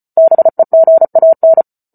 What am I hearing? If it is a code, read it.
Morse code, heard as a keyed tone, about 34 wpm, 650 Hz. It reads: BEGAN
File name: Book Two / 08-Chapter 8 - 34 WPM